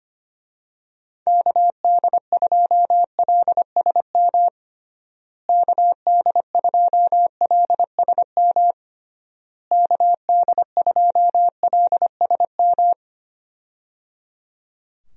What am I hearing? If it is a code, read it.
Morse code, heard as a keyed tone, about 25 wpm, 700 Hz. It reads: KD2LSM KD2LSM KD2LSM
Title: USA Call Signs - 6 Characters - Repeated 3x - 25wpm